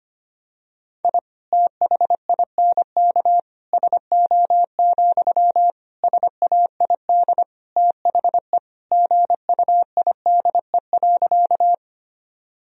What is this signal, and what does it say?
Morse code, heard as a keyed tone, about 25 wpm, 700 Hz. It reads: I THINK SO, SAID THE GUIDE.
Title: Sentences from Top 1000 Words - Review Part 3 - 25wpm